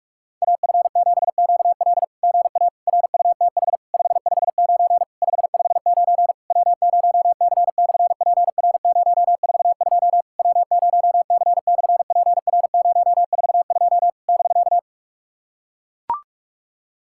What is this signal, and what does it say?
Morse code, heard as a keyed tone, about 44 wpm, 700 Hz. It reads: AF7QL GA RUTH 559 559 W0C/PR042 W0C/PR042 <BK>
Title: SOTA QSO Practice 45wpm